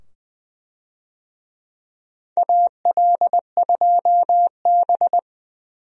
Morse code, heard as a keyed tone, about 20 wpm, 700 Hz. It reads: AL2B
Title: USA Call Signs - 4 Characters - 20wpm